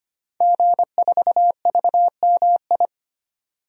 Morse code, heard as a keyed tone, about 25 wpm, 700 Hz. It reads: G4VMI
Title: DXCC-100 Call Signs - 5 Characters - 25wpm